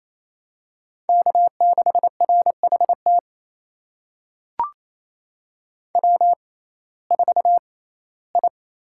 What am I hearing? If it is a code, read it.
Morse code, heard as a keyed tone, about 28 wpm, spaced as for 17 wpm, 700 Hz. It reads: K6RHT W4I
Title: USA Call Signs - 5 Characters - 6x Character Spacing - 28wpm